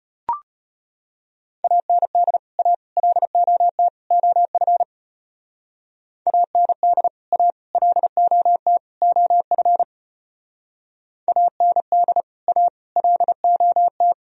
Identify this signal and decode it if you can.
Morse code, heard as a keyed tone, about 33 wpm, 700 Hz. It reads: AND A LOT OF AND A LOT OF AND A LOT